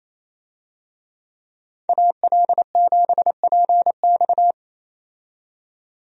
Morse code, heard as a keyed tone, about 28 wpm, 700 Hz. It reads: AL7PX
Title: USA Call Signs - 5 Characters - 28wpm